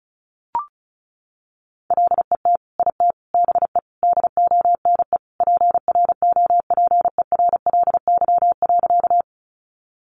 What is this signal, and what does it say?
Morse code, heard as a keyed tone, about 34 wpm, 700 Hz. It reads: LET IT BE DONE PROPERLY.
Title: Sentences from Top 1100 Words 35wpm